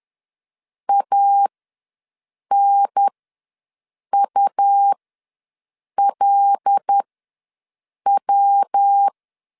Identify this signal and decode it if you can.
Morse code, heard as a keyed tone, about 10 wpm, spaced as for 6 wpm, 800 Hz. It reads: ANULW
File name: LEZIONI / 5 LBVC / 5 L ws 10wpm 3636